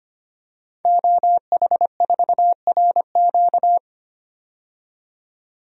Morse code, heard as a keyed tone, about 25 wpm, 700 Hz. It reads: OH4RQ
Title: DXCC-100 Call Signs - 5 Characters - 25wpm